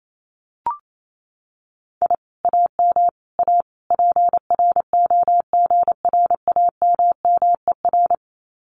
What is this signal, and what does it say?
Morse code, heard as a keyed tone, about 28 wpm, 700 Hz. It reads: I AM A PROGRAMMER
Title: Common QSO Phrases Encore 28wpm